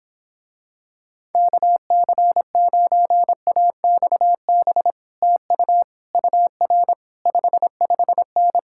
Morse code, heard as a keyed tone, about 26 wpm, 700 Hz. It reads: KC9AXB TU UR 55N